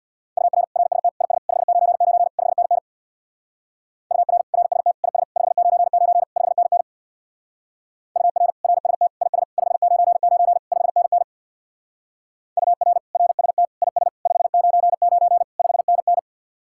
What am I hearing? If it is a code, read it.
Morse code, heard as a keyed tone, about 64 wpm, 700 Hz. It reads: UR RST IS 599 5NN UR RST IS 599 5NN UR RST IS 599 5NN UR RST IS 599 5NN